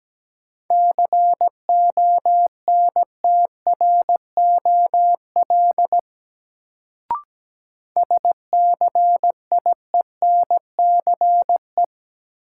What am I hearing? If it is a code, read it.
Morse code, heard as a keyed tone, about 17 wpm, 700 Hz. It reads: CONTROL SCIENCE
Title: Sentences from Top 1000 Words - Review Part 2 - New Word Drill - 17wpm